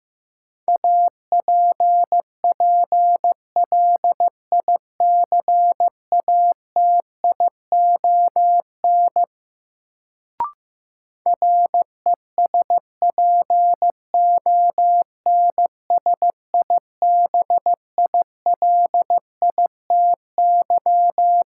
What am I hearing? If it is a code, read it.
Morse code, heard as a keyed tone, about 15 wpm, 700 Hz. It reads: APPLICATION RESPONSIBILITY
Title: Sentences from Top 1200 Words - New Word Drill - 15wpm